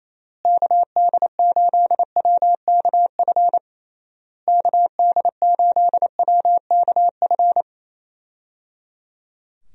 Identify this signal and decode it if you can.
Morse code, heard as a keyed tone, about 28 wpm, 700 Hz. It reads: KD8WKF KD8WKF
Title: USA Call Signs - 6 Characters - Repeated 3x - 28wpm